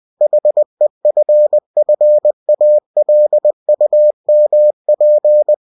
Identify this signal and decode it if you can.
Morse code, heard as a keyed tone, about 20 wpm, 600 Hz. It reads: HEFFALUMP
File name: cw-books / winnie-the-pooh / winnie-20wpm-page-0040